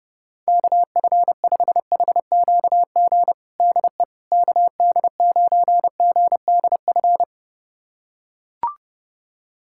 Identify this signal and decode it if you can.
Morse code, heard as a keyed tone, about 30 wpm, 700 Hz. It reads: KF5HQG DE KD9GDF
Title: USA Callsign DE Callsign 30wpm